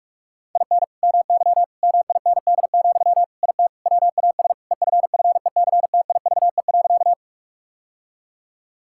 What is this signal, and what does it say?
Morse code, heard as a keyed tone, about 46 wpm, 700 Hz. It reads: IN MY MIND, IT WAS EFFECTIVE.
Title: Sentences from Top 1000 Words - Review Part 6 - 45wpm